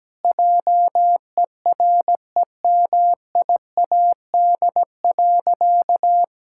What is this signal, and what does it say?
Morse code, heard as a keyed tone, about 17 wpm, 700 Hz. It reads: JEREMIAD.
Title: Pangram Challenge v2 - 17wpm